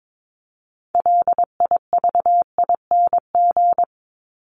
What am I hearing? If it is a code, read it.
Morse code, heard as a keyed tone, about 22 wpm, 700 Hz. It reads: LIVING
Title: ING Suffix - Top 100 Words with Base Words 22wpm